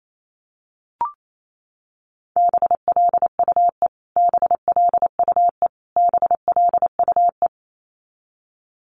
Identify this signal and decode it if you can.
Morse code, heard as a keyed tone, about 28 wpm, 700 Hz. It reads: BLUE BLUE BLUE
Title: Top 500 Words Repeated 28wpm